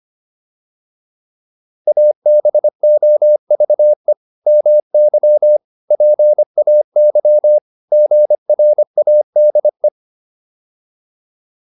Morse code, heard as a keyed tone, about 25 wpm, 600 Hz. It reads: ABOVE MY PAY GRADE